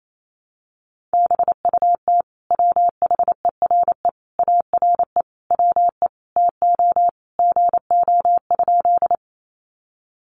Morse code, heard as a keyed tone, about 28 wpm, 700 Hz. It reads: BUT WHERE ARE WE TO GO?
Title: Sentences from Top 200 Words 28wpm